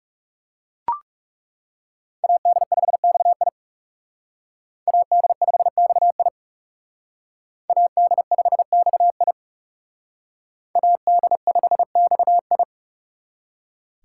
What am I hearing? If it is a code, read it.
Morse code, heard as a keyed tone, about 35 wpm, 700 Hz. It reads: AD5XI AD5XI AD5XI AD5XI
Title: Speed Racing - USA 2x2 Call Signs 30wpm